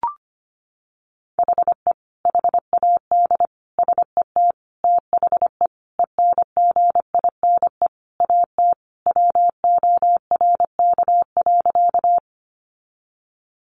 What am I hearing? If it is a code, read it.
Morse code, heard as a keyed tone, about 25 wpm, 700 Hz. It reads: HE HAD SET THE ENGINE AT WORK.
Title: Sentences from Top 1500 Words 25wpm